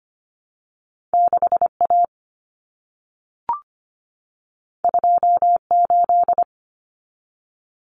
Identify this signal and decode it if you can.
Morse code, heard as a keyed tone, about 25 wpm, 700 Hz. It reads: 6A 28
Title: Common Cut Numbers - 2-Digit 25wpm